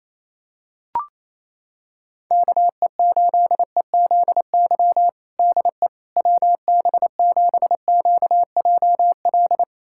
Morse code, heard as a keyed tone, about 28 wpm, 700 Hz. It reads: KE8EZY DE WB7QJL
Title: USA Callsign DE Callsign - 6 Characters - 28wpm